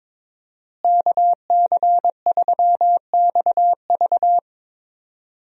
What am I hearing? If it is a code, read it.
Morse code, heard as a keyed tone, about 22 wpm, 700 Hz. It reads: KC3XV
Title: USA Call Signs - 5 Characters - Repeated 2x - 22wpm